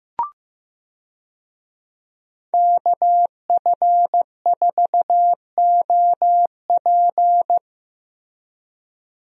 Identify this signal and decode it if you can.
Morse code, heard as a keyed tone, about 15 wpm, 700 Hz. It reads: KF4OP
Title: USA Call Signs - 5 Characters - Repeated 3x - 15wpm